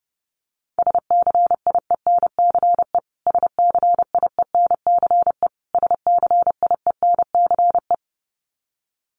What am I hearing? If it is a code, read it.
Morse code, heard as a keyed tone, about 30 wpm, 700 Hz. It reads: SCIENCE SCIENCE SCIENCE
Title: Top 500 Words Repeated 30wpm